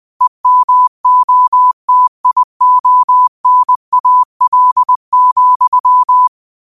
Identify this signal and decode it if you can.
Morse code, heard as a keyed tone, about 20 wpm, 1000 Hz. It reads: EMOTIONAL,